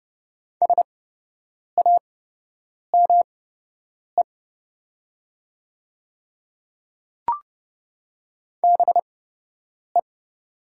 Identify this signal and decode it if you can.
Morse code, heard as a keyed tone, about 30 wpm, spaced as for 8 wpm, 700 Hz. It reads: SAME BE
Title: 8x Character Spacing - Top 200 Words - 30wpm